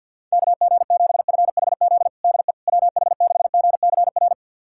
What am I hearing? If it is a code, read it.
Morse code, heard as a keyed tone, about 51 wpm, 690 Hz. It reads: KG7FHZ DE WH6GCR
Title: USA Callsign DE Callsign - 6 Characters - 50wpm